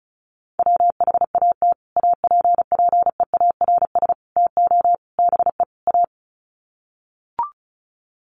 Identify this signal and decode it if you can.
Morse code, heard as a keyed tone, about 34 wpm, 700 Hz. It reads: WHAT APPEARS TO BE A
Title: Sets of 5 Words Encore2 35wpm